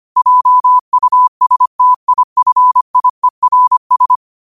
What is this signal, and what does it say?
Morse code, heard as a keyed tone, about 25 wpm, 1000 Hz. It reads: JUSTIFIERS
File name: qotdfiles / 25wpm